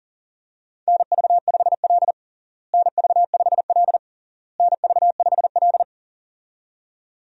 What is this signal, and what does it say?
Morse code, heard as a keyed tone, about 37 wpm, 700 Hz. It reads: NV5L NV5L NV5L
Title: USA Call Signs - 4 Characters - Repeated 3x - 40wpm